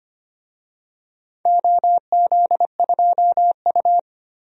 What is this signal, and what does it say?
Morse code, heard as a keyed tone, about 25 wpm, 700 Hz. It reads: OZ2U
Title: DXCC-100 Call Signs - 4 Characters - Repeated 2x - 25wpm